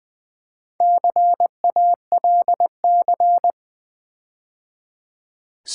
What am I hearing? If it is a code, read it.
Morse code, heard as a keyed tone, about 20 wpm, 700 Hz. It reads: CALC
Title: Build A Word - 4 Letters At A Time - 11-Letter Words - 20wpm